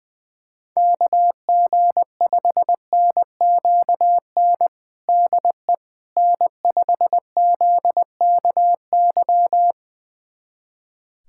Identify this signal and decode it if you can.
Morse code, heard as a keyed tone, about 21 wpm, 700 Hz. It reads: KG5NQN DE N5ZKY
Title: USA Callsign DE Callsign 20wpm